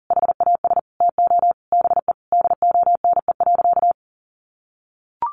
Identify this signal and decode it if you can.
Morse code, heard as a keyed tone, about 40 wpm, 700 Hz. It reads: HAS TO BE DONE.
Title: Sentences from Top 700 Words 40wpm